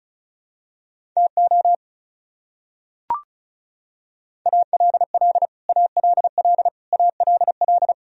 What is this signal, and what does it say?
Morse code, heard as a keyed tone, about 34 wpm, 700 Hz. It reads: TO ALL ALL ALL
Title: Top 100 Words Repeated 35wpm